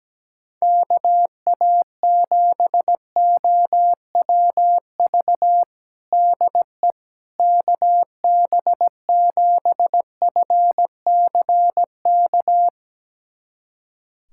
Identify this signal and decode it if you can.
Morse code, heard as a keyed tone, about 17 wpm, 700 Hz. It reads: KA7OWV DE KB7FCK
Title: USA Callsign DE Callsign - 6 Characters - 17wpm